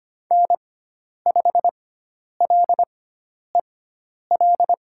Morse code, heard as a keyed tone, about 25 wpm, spaced as for 10 wpm, 700 Hz. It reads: N5LEL
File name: Lesson 16g - Letter c - Call Signs 25wpm